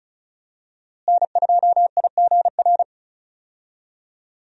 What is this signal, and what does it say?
Morse code, heard as a keyed tone, about 36 wpm, 700 Hz. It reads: N2IGR
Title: USA Call Signs - 5 Characters - Repeated 2x - 35wpm